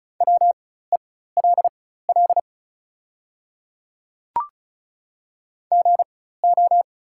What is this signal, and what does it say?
Morse code, heard as a keyed tone, about 36 wpm, spaced as for 16 wpm, 700 Hz. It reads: WELL GO
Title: 4x Character Spacing - Top 300 Words - 35wpm